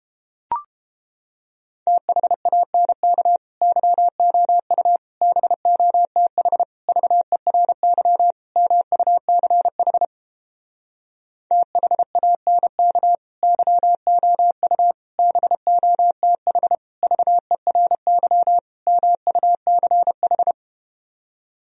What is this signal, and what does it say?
Morse code, heard as a keyed tone, about 31 wpm, 700 Hz. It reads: THANK YOU BOTH VERY MUCH THANK YOU BOTH VERY MUCH